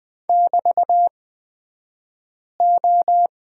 Morse code, heard as a keyed tone, about 20 wpm, spaced as for 5 wpm, 700 Hz. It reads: <BT>O
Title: 249 Lesson 40d - BK - Two Characters 20wpm